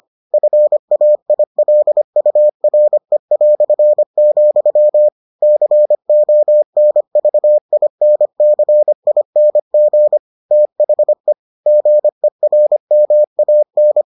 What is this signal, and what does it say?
Morse code, heard as a keyed tone, about 25 wpm, 600 Hz. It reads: FAILURE", CONVINCING THE GERMAN